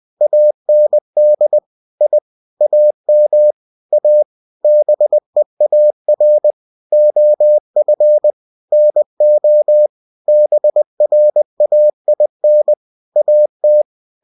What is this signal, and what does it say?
Morse code, heard as a keyed tone, about 20 wpm, 600 Hz. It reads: AND I AM A BEAR OF NO BRAIN AT
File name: cw-books / winnie-the-pooh / winnie-20wpm-page-0025